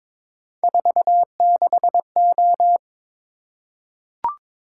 Morse code, heard as a keyed tone, about 22 wpm, 700 Hz. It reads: 46O
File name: practice / 22wpm / Three Letter-Number Combinations - 22wpm